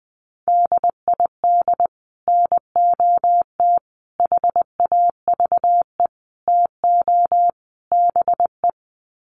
Morse code, heard as a keyed tone, about 20 wpm, 700 Hz. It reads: DID NOT HAVE TO BE
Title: Sets of 5 Words Encore2 20wpm